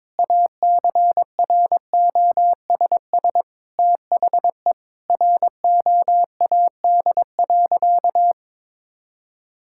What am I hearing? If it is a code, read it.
Morse code, heard as a keyed tone, about 22 wpm, 700 Hz. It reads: ACROSS THE ROAD.
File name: practice / 22wpm / Sentences with 2-Word Phrases - 22wpm